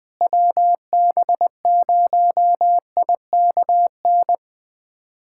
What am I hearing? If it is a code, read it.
Morse code, heard as a keyed tone, about 21 wpm, 700 Hz. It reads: WB0IKN
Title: USA Callsign DE Callsign 20wpm